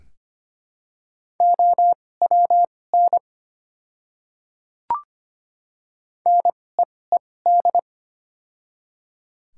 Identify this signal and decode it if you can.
Morse code, heard as a keyed tone, about 25 wpm, spaced as for 18 wpm, 700 Hz. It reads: OWN NEED
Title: 2x Character Spacing - Top 300 Words - 25wpm